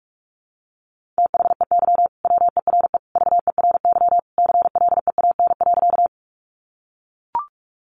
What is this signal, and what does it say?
Morse code, heard as a keyed tone, about 44 wpm, 700 Hz. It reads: THEY WERE VERY CLEAN.
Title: Sentences from Top 1100 Words 45wpm